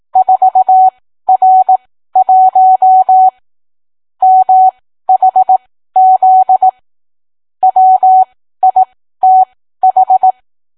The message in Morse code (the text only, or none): 4R1 MHZ WITH